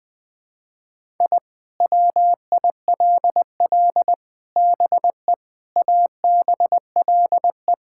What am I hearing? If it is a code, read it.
Morse code, heard as a keyed tone, about 20 wpm, 700 Hz. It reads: I WILL BE ABLE
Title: Sentences with 4-Word Phrases - 20wpm